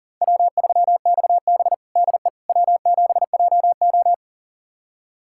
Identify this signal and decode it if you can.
Morse code, heard as a keyed tone, about 37 wpm, 700 Hz. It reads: W3XB DE W7JO
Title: USA Callsign DE Callsign - 4 Characters - 40wpm